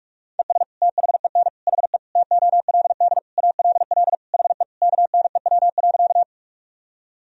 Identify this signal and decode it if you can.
Morse code, heard as a keyed tone, about 41 wpm, 700 Hz. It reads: ES THEN HE TOLD ALL HE KNEW.
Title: Sentences from Top 400 Words and Abbreviations 45wpm